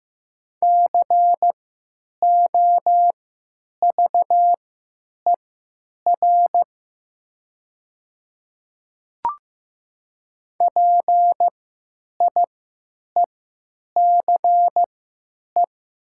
Morse code, heard as a keyed tone, about 15 wpm, spaced as for 9 wpm, 700 Hz. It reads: COVER PIECE
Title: 3x Character Spacing - Top 500 Words - 15wpm